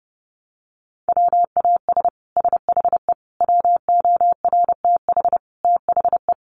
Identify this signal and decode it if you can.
Morse code, heard as a keyed tone, about 30 wpm, 700 Hz. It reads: WAS SHE WORTH THE
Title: Sentences from Top 1200 Words 30wpm